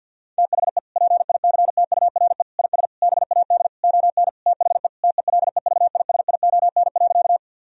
Morse code, heard as a keyed tone, about 52 wpm, 700 Hz. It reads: THE PICTURE IS BAD ON THE TELEVISION.
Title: Sentences from Top 1000 Words - Review Part 4 - 50wpm